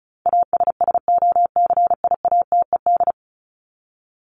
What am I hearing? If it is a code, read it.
Morse code, heard as a keyed tone, about 34 wpm, 700 Hz. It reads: ASSOCIATED
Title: ED Suffix - Top 100 Words with Base Words 35wpm